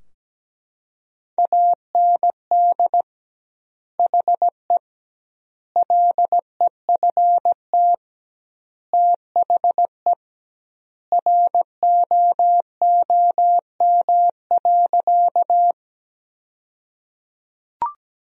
Morse code, heard as a keyed tone, about 17 wpm, 700 Hz. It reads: AND HE LEFT THE ROOM.
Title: 2x Word-Spacing - Sentences from Top 300 Words - 17wpm